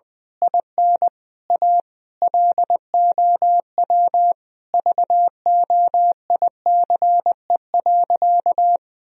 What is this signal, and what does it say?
Morse code, heard as a keyed tone, about 20 wpm, 700 Hz. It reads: IN A LOW VOICE.
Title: Sentences from Top 1000 Words - Review Part 1 - 20wpm